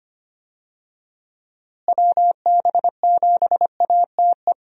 Morse code, heard as a keyed tone, about 25 wpm, 700 Hz. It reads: WB7ATE
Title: USA Call Signs - 6 Characters - Repeated 2x - 25wpm